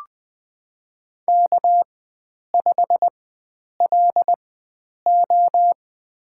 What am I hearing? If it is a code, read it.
Morse code, heard as a keyed tone, about 20 wpm, spaced as for 9 wpm, 700 Hz. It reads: K5LO
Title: USA Call Signs - 4 Characters - 4x Character Spacing - 20wpm